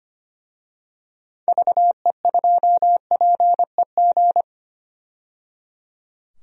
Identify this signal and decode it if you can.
Morse code, heard as a keyed tone, about 25 wpm, 700 Hz. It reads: VE2PEG
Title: DXCC-100 Call Signs - 6 Characters - 25wpm